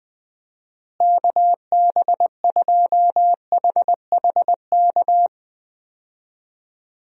Morse code, heard as a keyed tone, about 20 wpm, 700 Hz. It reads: KB2HHK
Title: USA Call Signs - 6 Characters - Repeated 2x - 20wpm